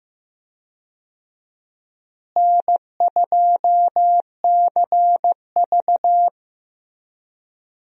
Morse code, heard as a keyed tone, about 15 wpm, 700 Hz. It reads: N2CV
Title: USA Call Signs - 4 Characters - Repeated 2x - 15wpm